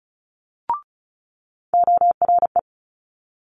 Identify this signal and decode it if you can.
Morse code, heard as a keyed tone, about 33 wpm, 700 Hz. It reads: ORE
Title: Multi-letter Phonemes Quick Fire 35wpm